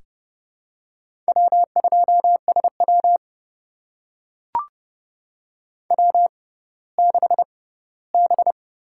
Morse code, heard as a keyed tone, about 30 wpm, spaced as for 17 wpm, 700 Hz. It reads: W2SW W6B
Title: USA Call Signs - 4 Characters - 6x Character Spacing - 30wpm